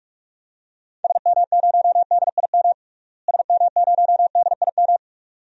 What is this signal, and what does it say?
Morse code, heard as a keyed tone, about 46 wpm, 700 Hz. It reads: SM0DIM SM0DIM
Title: DXCC-100 Call Signs - 6 Characters - Repeated 2x - 45wpm